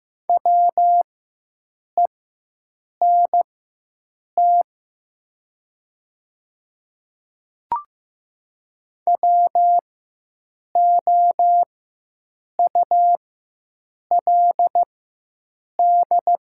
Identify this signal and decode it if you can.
Morse code, heard as a keyed tone, about 15 wpm, spaced as for 7 wpm, 700 Hz. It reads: WENT WOULD
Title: 4x Character Spacing - Top 500 Words - 15wpm